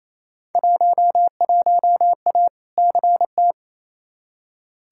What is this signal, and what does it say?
Morse code, heard as a keyed tone, about 28 wpm, 700 Hz. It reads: 11A CT
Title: Field Day Exchanges Encore - 28wpm